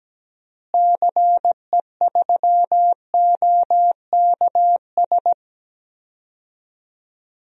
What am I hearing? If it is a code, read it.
Morse code, heard as a keyed tone, about 17 wpm, 700 Hz. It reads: CE3OKS
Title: DXCC-100 Call Signs - 6 Characters - Repeated 2x - 17wpm